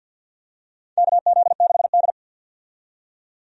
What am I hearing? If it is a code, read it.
Morse code, heard as a keyed tone, about 50 wpm, 700 Hz. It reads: KZ6D
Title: USA Call Signs - 4 Characters - 50wpm